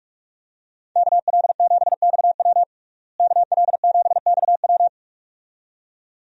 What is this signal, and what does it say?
Morse code, heard as a keyed tone, about 41 wpm, 700 Hz. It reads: KL7XW KL7XW
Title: DXCC-100 Call Signs - 5 Characters - Repeated 2x - 45wpm